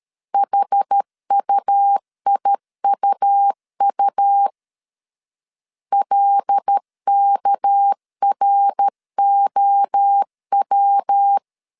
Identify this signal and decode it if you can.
Morse code, heard as a keyed tone, about 13 wpm, 800 Hz. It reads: HUIUU LKROW
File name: CW / DET / 1 6 1500C S B 12wpm 3431